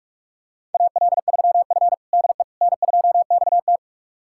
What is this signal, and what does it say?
Morse code, heard as a keyed tone, about 44 wpm, 690 Hz. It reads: AL3F DE N2XT
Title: USA Callsign DE Callsign - 4 Characters - 45wpm